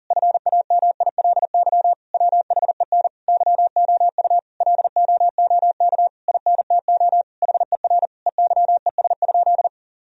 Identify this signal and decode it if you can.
Morse code, heard as a keyed tone, about 40 wpm, 700 Hz. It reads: FAMILY WHEN YOU LOOK INTO HER EYES?
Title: Sentences from Top 1000 Words - Review Part 4 - 40wpm